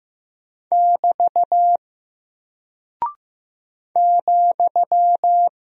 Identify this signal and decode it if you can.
Morse code, heard as a keyed tone, about 15 wpm, 700 Hz. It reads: <BT> ,
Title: Single Character (Letter:Number:Punctuation:Prosign) 15wpm